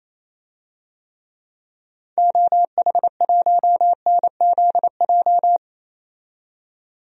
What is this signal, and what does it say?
Morse code, heard as a keyed tone, about 28 wpm, 700 Hz. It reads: OH1NZJ
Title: DXCC-100 Call Signs - 6 Characters - 28wpm